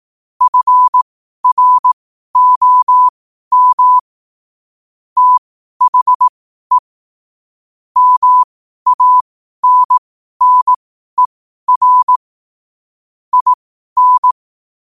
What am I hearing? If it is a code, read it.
Morse code, heard as a keyed tone, about 18 wpm, spaced as for 12 wpm, 1000 Hz. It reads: FROM THE MANNER IN
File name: qotdfiles / 12wpm